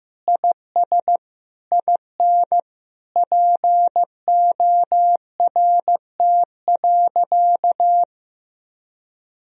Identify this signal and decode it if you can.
Morse code, heard as a keyed tone, about 15 wpm, 700 Hz. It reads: IS IN PORT.
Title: Sentences from Top 1000 Words - Review Part 1 - 15wpm